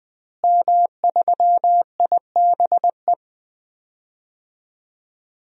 Morse code, heard as a keyed tone, about 20 wpm, 700 Hz. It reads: M3IBE
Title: DXCC-100 Call Signs - 5 Characters - Repeated 3x - 20wpm